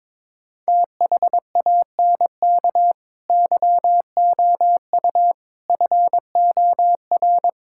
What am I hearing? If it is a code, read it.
Morse code, heard as a keyed tone, about 22 wpm, 700 Hz. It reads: THANK YOU FOR
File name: Sentences with 7-Word Phrases - 22wpm